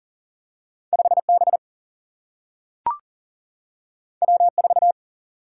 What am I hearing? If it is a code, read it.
Morse code, heard as a keyed tone, about 40 wpm, 700 Hz. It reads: 5B W4
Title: Single-Letter Single-Number 40wpm